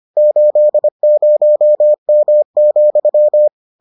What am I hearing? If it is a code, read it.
Morse code, heard as a keyed tone, about 25 wpm, 600 Hz. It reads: 80M,